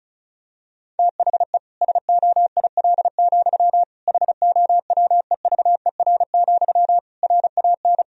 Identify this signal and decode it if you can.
Morse code, heard as a keyed tone, about 36 wpm, 700 Hz. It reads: THE SOIL, HOWEVER, RAN
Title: Sentences from Top 1000 Words - Review Part 4 - 35wpm